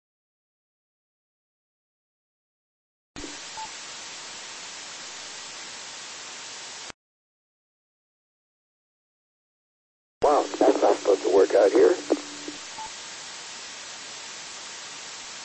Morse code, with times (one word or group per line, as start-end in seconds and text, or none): none